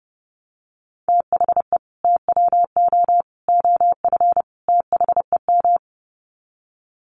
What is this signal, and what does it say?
Morse code, heard as a keyed tone, about 30 wpm, 700 Hz. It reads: THE TWO OF THEM
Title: Sets of 4 Words 30wpm